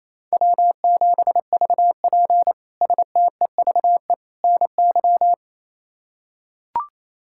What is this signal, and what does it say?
Morse code, heard as a keyed tone, about 27 wpm, 700 Hz. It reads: W7VP STEVE NY
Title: K1USN SST QSO Practice - 28wpm